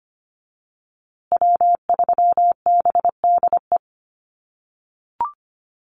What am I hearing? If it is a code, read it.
Morse code, heard as a keyed tone, about 25 wpm, 700 Hz. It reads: W3BDE
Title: USA Call Signs Repeated 2x 25wpm